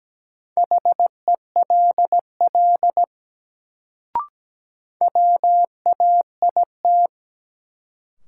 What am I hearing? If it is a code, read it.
Morse code, heard as a keyed tone, about 17 wpm, 700 Hz. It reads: HELL WAIT
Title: Sets of 5 Words Encore - New Word Drill - 17wpm